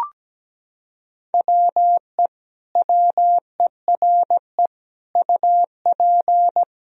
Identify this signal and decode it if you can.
Morse code, heard as a keyed tone, about 17 wpm, 700 Hz. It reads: WE WERE UP
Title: Sentences from Top 1000 Words - Review Part 2 - 17wpm